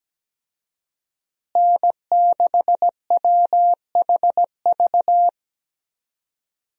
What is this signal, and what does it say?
Morse code, heard as a keyed tone, about 17 wpm, 700 Hz. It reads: N6WHV
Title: USA Call Signs - 5 Characters - Repeated 2x - 17wpm